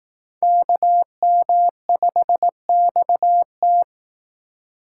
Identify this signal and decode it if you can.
Morse code, heard as a keyed tone, about 18 wpm, 700 Hz. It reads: KM5XT
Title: Speed Racing - USA 2x2 Call Signs 15wpm